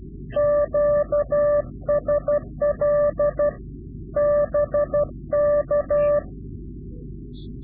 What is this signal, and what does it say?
Morse code, heard as a keyed tone, about 12 wpm, 590 Hz. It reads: QSL BK
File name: CW-QSO / Directional Call / Directional QSO